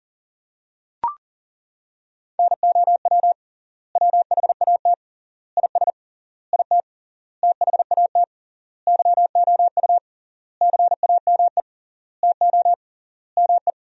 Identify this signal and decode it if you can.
Morse code, heard as a keyed tone, about 40 wpm, 700 Hz. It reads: NOW WHAT IS IT THAT YOU CAME TO ME
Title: 3x Word-Spacing - Sentences from Top 200 Words - 40wpm